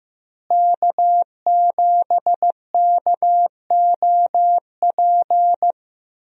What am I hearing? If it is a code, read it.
Morse code, heard as a keyed tone, about 15 wpm, 700 Hz. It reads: K7KOP
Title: USA Call Signs - 5 Characters - Repeated 2x - 15wpm